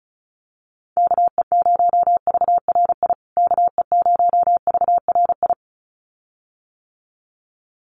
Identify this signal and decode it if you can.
Morse code, heard as a keyed tone, about 34 wpm, 700 Hz. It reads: KE0VRI KE0VRI
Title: USA Call Signs Repeated 2x 35wpm